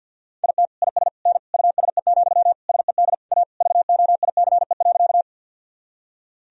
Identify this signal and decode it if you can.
Morse code, heard as a keyed tone, about 52 wpm, 700 Hz. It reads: IT IS N USE, SED A VOICE.
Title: Sentences from Top 500 Words and Abbreviations 50wpm